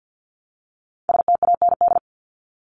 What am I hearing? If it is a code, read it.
Morse code, heard as a keyed tone, about 48 wpm, 700 Hz. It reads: STAND
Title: Top 300 Words 50wpm